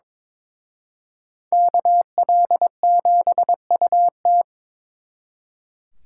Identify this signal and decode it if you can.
Morse code, heard as a keyed tone, about 26 wpm, 700 Hz. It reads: KL7UT
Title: Speed Racing - USA 2x2 Call Signs 22wpm